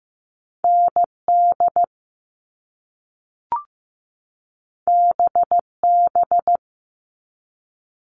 Morse code, heard as a keyed tone, about 15 wpm, 700 Hz. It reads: ND BB
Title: All Cut Numbers - 2-Digit 15wpm